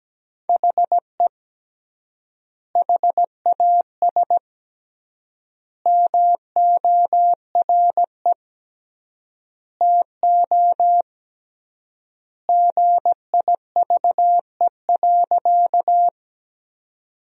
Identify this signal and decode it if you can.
Morse code, heard as a keyed tone, about 17 wpm, 700 Hz. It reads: HE HAS MORE TO GIVE.
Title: 3x Word-Spacing - Sentences from Top 200 Words - 17wpm